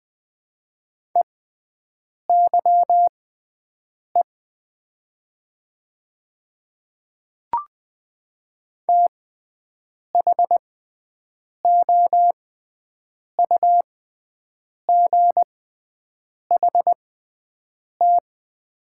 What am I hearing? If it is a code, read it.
Morse code, heard as a keyed tone, about 20 wpm, spaced as for 7 wpm, 700 Hz. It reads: EYE THOUGHT
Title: 6x Character Spacing - Top 300 Words - 20wpm